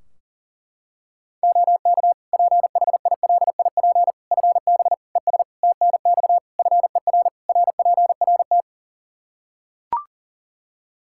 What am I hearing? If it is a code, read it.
Morse code, heard as a keyed tone, about 40 wpm, 700 Hz. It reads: OK PHILIP FB ES TNX FER RPRT
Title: K1USN SST QSO Practice 40wpm